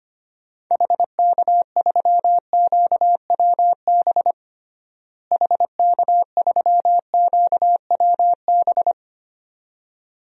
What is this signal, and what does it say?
Morse code, heard as a keyed tone, about 25 wpm, 700 Hz. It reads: HK3QWB HK3QWB
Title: DXCC-100 Call Signs - 6 Characters - Repeated 2x - 25wpm